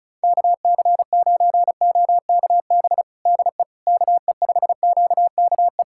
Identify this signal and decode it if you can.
Morse code, heard as a keyed tone, about 34 wpm, 700 Hz. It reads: KC9OKB DE KE5QKE
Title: USA Callsign DE Callsign - 6 Characters - 35wpm